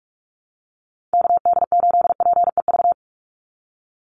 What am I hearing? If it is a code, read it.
Morse code, heard as a keyed tone, about 44 wpm, 700 Hz. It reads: KD8PEV
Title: USA Call Signs 45wpm